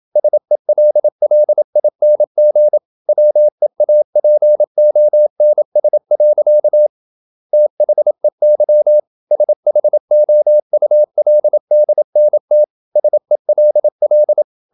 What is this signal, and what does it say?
Morse code, heard as a keyed tone, about 27 wpm, 600 Hz. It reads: SELLING WEAPONS. THEY SHOULDNT SELL
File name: Never 0027wpm